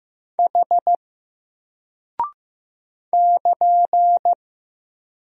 Morse code, heard as a keyed tone, about 15 wpm, 700 Hz. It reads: H <KN>
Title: Single Character (Letter:Number:Punctuation:Prosign) Alt 15wpm